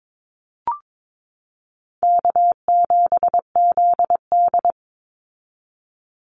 Morse code, heard as a keyed tone, about 22 wpm, 700 Hz. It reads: K7ZD
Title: Easy Call Signs Repeated 3x - 22wpm